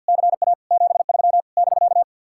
Morse code, heard as a keyed tone, about 66 wpm, spaced as for 58 wpm, 700 Hz. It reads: CA 73 <BK>
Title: Speed-Racing Common QSO Phrases - 50wpm